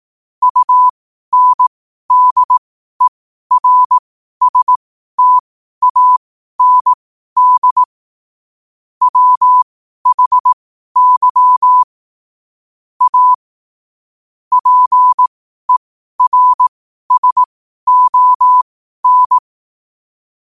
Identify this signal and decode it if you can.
Morse code, heard as a keyed tone, about 18 wpm, spaced as for 12 wpm, 1000 Hz. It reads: UNDERSTAND WHY A PERSON
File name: qotdfiles / 12wpm